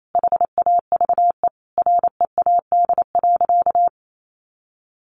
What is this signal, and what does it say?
Morse code, heard as a keyed tone, about 28 wpm, 700 Hz. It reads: HAVE READ.
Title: Sentences from Top 1000 Words 28wpm